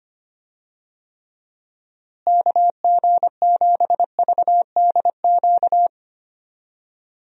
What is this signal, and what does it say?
Morse code, heard as a keyed tone, about 25 wpm, 700 Hz. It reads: KG7VDQ
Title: USA Call Signs - 6 Characters - Repeated 2x - 25wpm